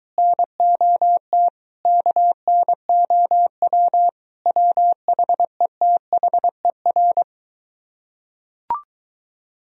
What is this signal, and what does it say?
Morse code, heard as a keyed tone, about 22 wpm, 700 Hz. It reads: NOT KNOW WHETHER